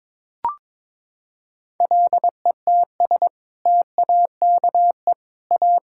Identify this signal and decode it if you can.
Morse code, heard as a keyed tone, about 22 wpm, 700 Hz. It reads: LETS TAKE A
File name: practice / 22wpm / Sentences with 2-Word Phrases - 22wpm